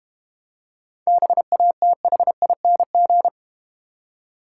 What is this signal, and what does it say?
Morse code, heard as a keyed tone, about 32 wpm, 700 Hz. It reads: BATHING